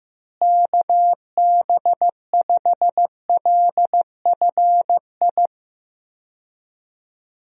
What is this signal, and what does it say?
Morse code, heard as a keyed tone, about 14 wpm, 700 Hz. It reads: KB5LFI
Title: USA Call Signs - 6 Characters - Repeated 3x - 15wpm